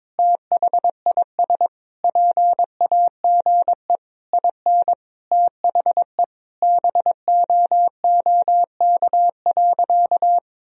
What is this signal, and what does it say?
Morse code, heard as a keyed tone, about 22 wpm, 700 Hz. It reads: THIS PAGE IN THE BOOK.
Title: Sentences from Top 1000 Words - Review Part 1 - 22wpm